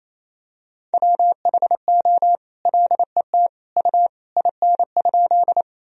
Morse code, heard as a keyed tone, about 28 wpm, 700 Hz. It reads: WHO LET U IN?
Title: Sentences from Top 300 Words and Abbreviations 28wpm